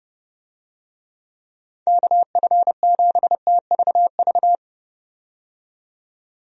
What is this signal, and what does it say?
Morse code, heard as a keyed tone, about 30 wpm, 700 Hz. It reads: KF7TVV
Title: USA Call Signs - 6 Characters - 30wpm